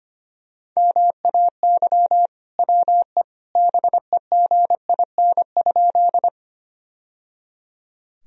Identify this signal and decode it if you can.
Morse code, heard as a keyed tone, about 25 wpm, 700 Hz. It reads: MAY WE BEGIN?
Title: Sentences from Top 1000 Words - Review Part 1 - 25wpm